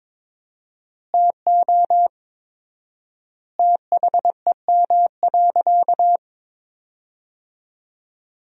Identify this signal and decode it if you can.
Morse code, heard as a keyed tone, about 22 wpm, 700 Hz. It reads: TO THEM.
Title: 4x Word-Spacing - Sentences from Top 500 Words - 22wpm